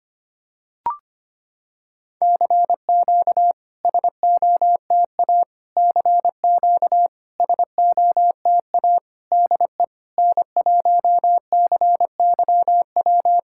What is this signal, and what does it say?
Morse code, heard as a keyed tone, about 25 wpm, 690 Hz. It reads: CQ SOTA CQ SOTA DE N1CYW
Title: SOTA QSO Practice 25wpm